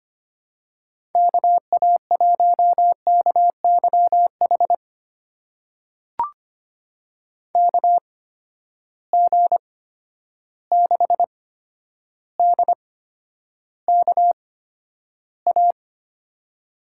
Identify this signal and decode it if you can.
Morse code, heard as a keyed tone, about 25 wpm, spaced as for 11 wpm, 700 Hz. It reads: KA1KYH KG6DKA
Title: USA Call Signs - 6 Characters - 8x Character Spacing - 25wpm